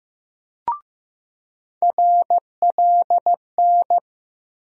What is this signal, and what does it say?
Morse code, heard as a keyed tone, about 15 wpm, 700 Hz. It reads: RLN
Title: USA Call Sign Postfixes - 3 Letters 15wpm